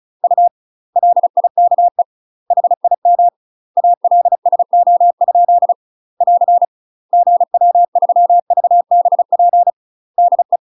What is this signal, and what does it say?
Morse code, heard as a keyed tone, about 34 wpm, 700 Hz. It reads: U LIKE HIM ALSO? <AR> GW3VBP DE
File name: UK MASTER QSOs TRAINING / 35WPM / UK MASTER QSOs 0036